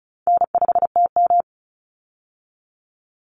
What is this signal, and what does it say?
Morse code, heard as a keyed tone, about 34 wpm, 700 Hz. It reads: N5TM
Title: Easy Call Signs Repeated 3x - 35wpm